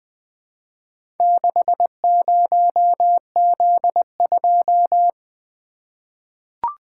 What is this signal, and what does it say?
Morse code, heard as a keyed tone, about 20 wpm, 700 Hz. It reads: 60Z2
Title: Four Letter-Number Combinations - 20wpm